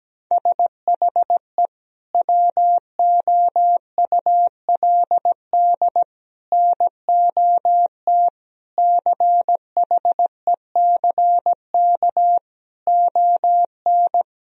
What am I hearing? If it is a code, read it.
Morse code, heard as a keyed tone, about 17 wpm, 700 Hz. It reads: SHE WOULD NOT CHECK ON
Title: Sentences from Top 1000 Words - Review Part 3 - 17wpm